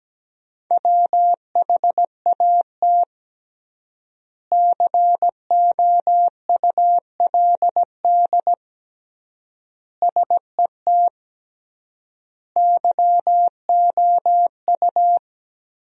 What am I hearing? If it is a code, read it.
Morse code, heard as a keyed tone, about 17 wpm, 700 Hz. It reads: WHAT COULD SET YOU
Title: 3x Word-Spacing - Sentences from Top 200 Words - 17wpm